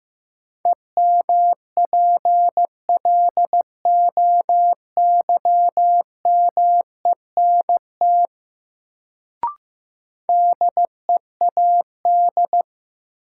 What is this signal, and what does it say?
Morse code, heard as a keyed tone, about 15 wpm, 700 Hz. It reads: EMPLOYMENT DEAD
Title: Sentences from Top 1500 Words - New Word Drill Alt - 15wpm